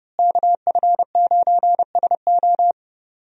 Morse code, heard as a keyed tone, about 30 wpm, 700 Hz. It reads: KF9SO
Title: USA Call Signs - 5 Characters - Repeated 3x - 30wpm